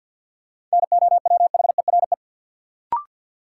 Morse code, heard as a keyed tone, about 52 wpm, 700 Hz. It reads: NOWHERE
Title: Sets of 5 Words Encore - New Word Drill - 50wpm